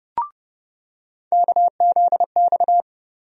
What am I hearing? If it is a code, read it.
Morse code, heard as a keyed tone, about 32 wpm, 700 Hz. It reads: KZX
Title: USA Call Sign Postfixes - 3 Letters 30wpm